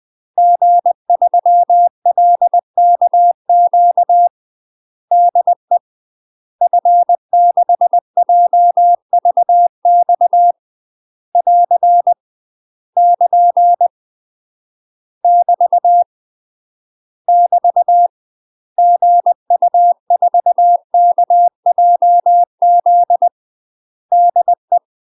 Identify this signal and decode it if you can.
Morse code, heard as a keyed tone, about 20 wpm, 700 Hz. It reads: G3LKQ DE F6JVX <AR> <KN> <BT> <BT> GU4KJZ DE
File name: UK MASTER QSOs TRAINING / 20WPM / UK MASTER QSOs 0036